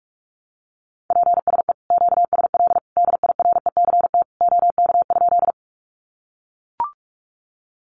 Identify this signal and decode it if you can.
Morse code, heard as a keyed tone, about 44 wpm, 700 Hz. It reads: PSE QSL DIRECT OK?
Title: Common QSO Phrases 45wpm